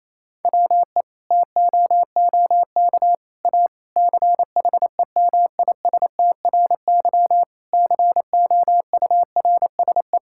Morse code, heard as a keyed tone, about 28 wpm, 700 Hz. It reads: WE TOOK A CHEMISTRY COURSE
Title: Sentences from Top 1000 Words - Review Part 6 - 28wpm